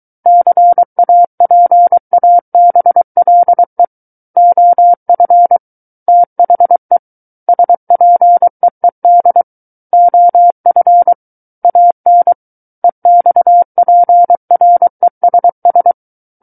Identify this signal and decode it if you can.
Morse code, heard as a keyed tone, about 23 wpm, 700 Hz. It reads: CAPABLE OF THE SPEED OF AN EXPRESS
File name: Book One / 14-Chapter 14 - 23 WPM